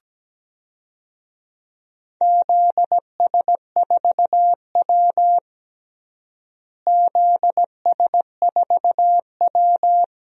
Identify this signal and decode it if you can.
Morse code, heard as a keyed tone, about 17 wpm, 700 Hz. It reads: ZS4W ZS4W
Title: DXCC-100 Call Signs - 4 Characters - Repeated 2x - 17wpm